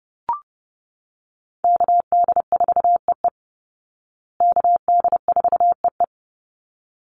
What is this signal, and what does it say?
Morse code, heard as a keyed tone, about 30 wpm, 700 Hz. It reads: KD4EE KD4EE
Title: Call Signs Repeated 3x - 30wpm